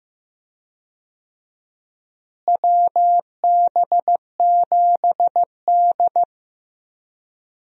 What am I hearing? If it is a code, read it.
Morse code, heard as a keyed tone, about 15 wpm, 700 Hz. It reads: WB7D
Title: USA Call Signs - 4 Characters - Repeated 3x - 15wpm